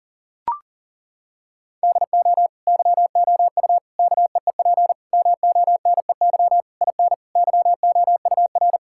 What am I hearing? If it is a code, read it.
Morse code, heard as a keyed tone, about 40 wpm, 700 Hz. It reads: DO YOU KEEP MONEY IN YOUR
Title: Sentences from Top 1000 Words - Review Part 6 - 40wpm